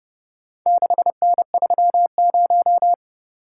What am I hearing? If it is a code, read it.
Morse code, heard as a keyed tone, about 29 wpm, 700 Hz. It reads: 6N30